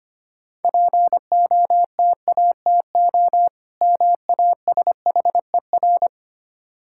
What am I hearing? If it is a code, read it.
Morse code, heard as a keyed tone, about 25 wpm, 700 Hz. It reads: POTATO MASHER
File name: Easy Words - Speed-Racing with Hints - 25wpm